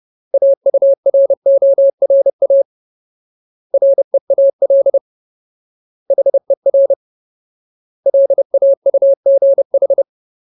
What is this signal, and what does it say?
Morse code, heard as a keyed tone, about 30 wpm, 550 Hz. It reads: AURORA REAL HER LAUGH
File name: Ultimate Word List / 30WPM Ultimate